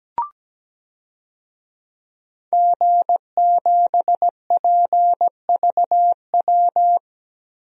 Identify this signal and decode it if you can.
Morse code, heard as a keyed tone, about 17 wpm, 700 Hz. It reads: G7PVW
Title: DXCC-100 Call Signs - 5 Characters - Repeated 3x - 17wpm